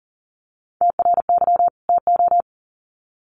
Morse code, heard as a keyed tone, about 40 wpm, 700 Hz. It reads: TRY TO
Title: Sets of 2 Words 40wpm